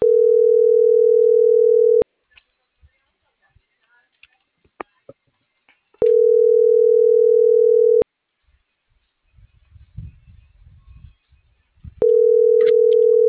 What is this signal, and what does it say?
Ringback tone (North America), 440 + 480 Hz.